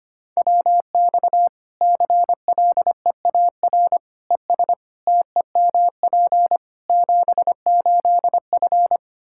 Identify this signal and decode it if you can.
Morse code, heard as a keyed tone, about 25 wpm, 700 Hz. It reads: WX CLEAR ES TEMP 78F
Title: K1USN SST QSO Practice 25wpm